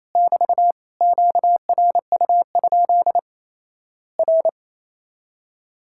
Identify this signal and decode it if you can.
Morse code, heard as a keyed tone, about 27 wpm, 700 Hz. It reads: <BT> QRU? R
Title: Worked All Europe DX Contest - QTC Exchange - 28wpm